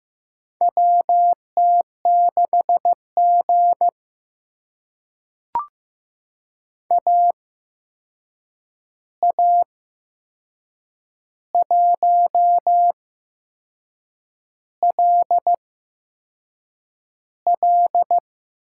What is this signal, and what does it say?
Morse code, heard as a keyed tone, about 15 wpm, spaced as for 6 wpm, 700 Hz. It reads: WT6G AA1LL
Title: USA Call Signs - 8x Character Spacing - 15wpm